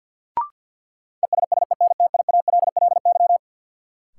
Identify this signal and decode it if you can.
Morse code, heard as a keyed tone, about 53 wpm, 700 Hz. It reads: ESSENTIALLY
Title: Sentences from Top 1300 Words - New Word Drill Alt - 50wpm